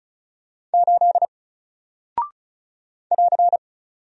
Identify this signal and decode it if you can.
Morse code, heard as a keyed tone, about 36 wpm, 700 Hz. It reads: 8 <AR>
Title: Single Character (Letter:Number:Punctuation:Prosign) Alt 35wpm